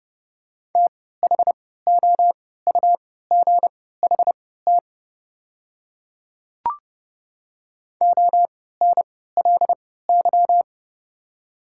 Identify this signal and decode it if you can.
Morse code, heard as a keyed tone, about 31 wpm, spaced as for 17 wpm, 690 Hz. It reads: THOUGHT ONLY
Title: 3x Character Spacing - Top 300 Words - 30wpm